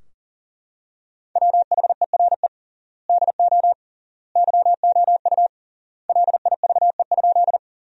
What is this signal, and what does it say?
Morse code, heard as a keyed tone, about 40 wpm, 700 Hz. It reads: WHERE DO YOU LIVE?
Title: 3x Word-Spacing - Sentences from Top 200 Words - 40wpm